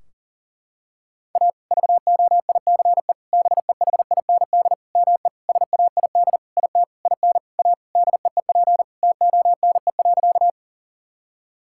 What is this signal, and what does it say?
Morse code, heard as a keyed tone, about 40 wpm, 700 Hz. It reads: A VOICE BEHIND ME SAID IT IN A DEEP TONE.
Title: Sentences from Top 1600 Words 40wpm